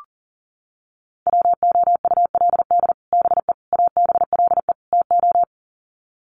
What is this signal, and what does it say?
Morse code, heard as a keyed tone, about 40 wpm, 700 Hz. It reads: WOULD BE ABLE TO
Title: Sets of 4 Words 40wpm